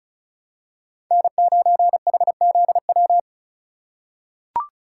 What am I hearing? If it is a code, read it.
Morse code, heard as a keyed tone, about 34 wpm, 700 Hz. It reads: N9HZW
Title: USA Call Signs - 5 Characters - 4x Character Spacing - 35wpm